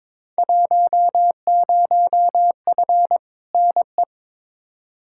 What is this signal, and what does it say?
Morse code, heard as a keyed tone, about 22 wpm, 700 Hz. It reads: 10F NE
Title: Field Day Exchanges Encore - 22wpm